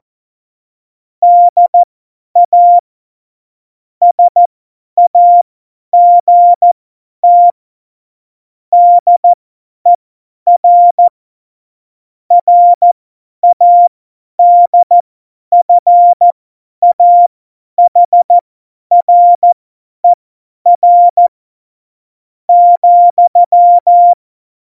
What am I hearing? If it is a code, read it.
Morse code, heard as a keyed tone, about 14 wpm, spaced as for 10 wpm, 700 Hz. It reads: DA SAGT DER RADFAHRER,
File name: Morsekurs / Witze / Witz 05. Februar 2019, 50 F70